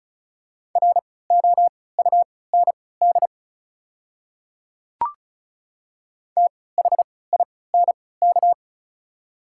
Morse code, heard as a keyed tone, about 34 wpm, spaced as for 20 wpm, 700 Hz. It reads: ROUND THINK
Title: 3x Character Spacing - Top 200 Words - 35wpm